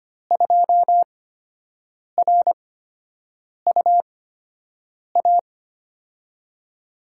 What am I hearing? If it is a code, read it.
Morse code, heard as a keyed tone, about 25 wpm, spaced as for 7 wpm, 700 Hz. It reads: 2RUA
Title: USA Call Signs - 8x Character Spacing - 25wpm